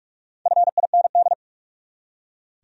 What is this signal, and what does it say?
Morse code, heard as a keyed tone, about 42 wpm, 700 Hz. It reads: FIND
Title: Top 100 Words 45wpm